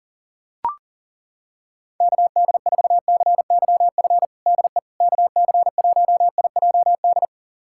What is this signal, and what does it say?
Morse code, heard as a keyed tone, about 40 wpm, 700 Hz. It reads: KD4CYF DE KC1IJD
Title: USA Callsign DE Callsign - 6 Characters - 40wpm